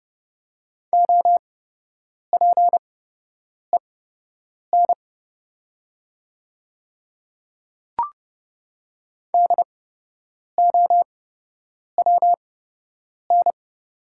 Morse code, heard as a keyed tone, about 30 wpm, spaced as for 8 wpm, 700 Hz. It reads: OPEN DOWN
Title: 8x Character Spacing - Top 300 Words - 30wpm